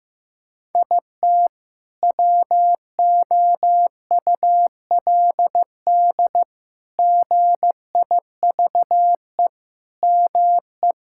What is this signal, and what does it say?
Morse code, heard as a keyed tone, about 15 wpm, 700 Hz. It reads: IT WOULD GIVE ME
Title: Sentences from Top 1600 Words 15wpm